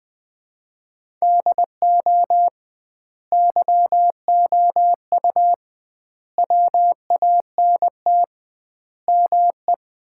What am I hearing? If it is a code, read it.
Morse code, heard as a keyed tone, about 20 wpm, 700 Hz. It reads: DO YOU WANT ME
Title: 2x Word-Spacing - Sets of 5 Words - 20wpm